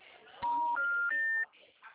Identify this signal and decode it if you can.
Special information tone, with speech.
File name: out- -2010- - - .64667